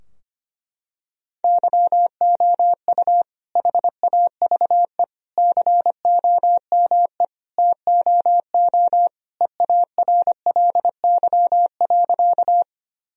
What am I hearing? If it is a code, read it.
Morse code, heard as a keyed tone, about 25 wpm, 700 Hz. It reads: YOU HAVE COME TOO EARLY.
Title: Sentences from Top 1000 Words - Review Part 2 - 25wpm